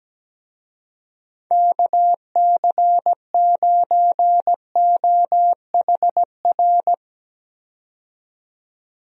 Morse code, heard as a keyed tone, about 17 wpm, 700 Hz. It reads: KC9OHR
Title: USA Call Signs - 6 Characters - 17wpm